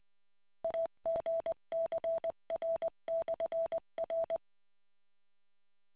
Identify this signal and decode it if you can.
Morse code, heard as a keyed tone, about 20 wpm, 660 Hz. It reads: AGCR/R